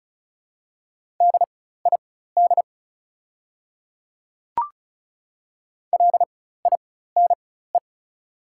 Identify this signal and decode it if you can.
Morse code, heard as a keyed tone, about 36 wpm, spaced as for 16 wpm, 690 Hz. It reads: DID LINE
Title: 4x Character Spacing - Top 300 Words - 35wpm